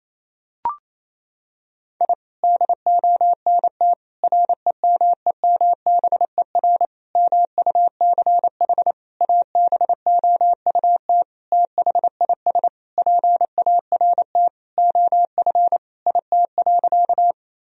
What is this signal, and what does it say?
Morse code, heard as a keyed tone, about 28 wpm, 700 Hz. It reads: I DONT REMEMBER MUCH ABOUT THIS PART OF IT.
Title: Sentences from Top 1000 Words - Review Part 1 - 28wpm